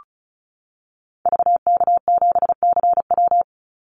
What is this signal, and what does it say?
Morse code, heard as a keyed tone, about 34 wpm, 700 Hz. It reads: VK7CW
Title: Call Signs Repeated 2x - 35wpm